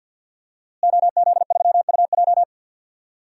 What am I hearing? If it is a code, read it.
Morse code, heard as a keyed tone, about 53 wpm, 700 Hz. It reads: OZ3UJ
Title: DXCC-100 Call Signs - 5 Characters - Repeated 2x - 50wpm